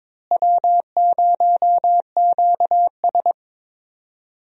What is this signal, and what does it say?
Morse code, heard as a keyed tone, about 22 wpm, 700 Hz. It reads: W0QS
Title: USA Call Signs - 4 Characters - Repeated 3x - 22wpm